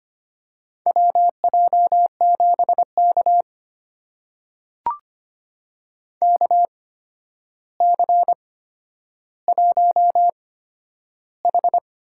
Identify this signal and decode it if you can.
Morse code, heard as a keyed tone, about 25 wpm, spaced as for 11 wpm, 700 Hz. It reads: WJ7K KC1H
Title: USA Call Signs - 4 Characters - 8x Character Spacing - 25wpm